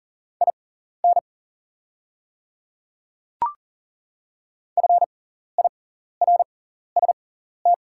Morse code, heard as a keyed tone, about 40 wpm, spaced as for 14 wpm, 690 Hz. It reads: IN FIRST
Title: 6x Character Spacing - Top 500 Words - 40wpm